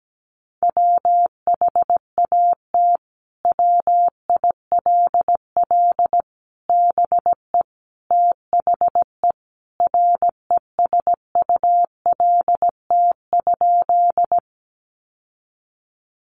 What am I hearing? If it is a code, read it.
Morse code, heard as a keyed tone, about 17 wpm, 700 Hz. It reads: WHAT WILL BE THE RESULT?
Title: Sentences from Top 700 Words 17wpm